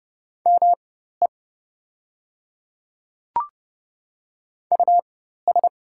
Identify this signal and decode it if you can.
Morse code, heard as a keyed tone, about 30 wpm, spaced as for 14 wpm, 700 Hz. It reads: ME US
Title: 4x Character Spacing - Top 200 Words - 30wpm